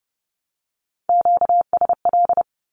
Morse code, heard as a keyed tone, about 30 wpm, 700 Hz. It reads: QSL
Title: Q-Codes 30wpm